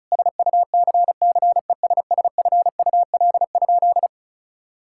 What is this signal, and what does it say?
Morse code, heard as a keyed tone, about 36 wpm, 700 Hz. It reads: SUCCESSFUL?
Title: Sentences from Top 1000 Words - Review Part 6 - 35wpm